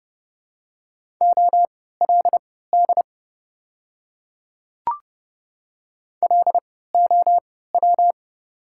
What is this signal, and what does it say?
Morse code, heard as a keyed tone, about 32 wpm, spaced as for 17 wpm, 700 Hz. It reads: OLD LOW
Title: 3x Character Spacing - Top 300 Words - 30wpm